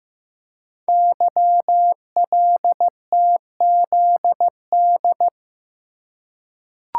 Morse code, heard as a keyed tone, about 15 wpm, 700 Hz. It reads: YLTZD
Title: Code Groups - 5 Letter:Numbers - 15wpm